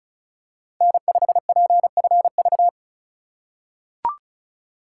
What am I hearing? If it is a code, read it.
Morse code, heard as a keyed tone, about 34 wpm, 700 Hz. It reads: N5PFV
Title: USA Call Signs - 5 Characters - 3x Character Spacing - 35wpm